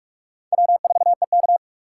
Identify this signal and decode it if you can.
Morse code, heard as a keyed tone, about 38 wpm, 700 Hz. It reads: W4EK